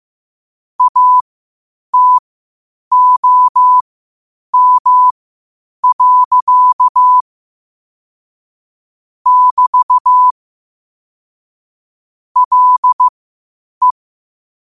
Morse code, heard as a keyed tone, about 15 wpm, spaced as for 8 wpm, 1000 Hz. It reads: ATOM. <BT> LE
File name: qotdfiles / 8wpm